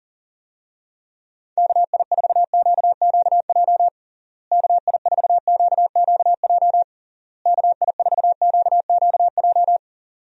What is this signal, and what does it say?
Morse code, heard as a keyed tone, about 40 wpm, 700 Hz. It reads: KI4QQJ KI4QQJ KI4QQJ
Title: USA Call Signs - 6 Characters - Repeated 3x - 40wpm